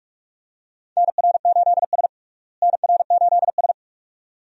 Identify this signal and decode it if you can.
Morse code, heard as a keyed tone, about 41 wpm, 700 Hz. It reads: NR8S NR8S
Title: USA Call Signs - 4 Characters - Repeated 2x - 45wpm